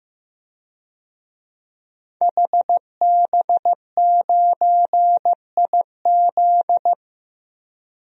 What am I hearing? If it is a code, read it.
Morse code, heard as a keyed tone, about 15 wpm, 700 Hz. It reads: HB9IZ
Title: DXCC-100 Call Signs - 5 Characters - 15wpm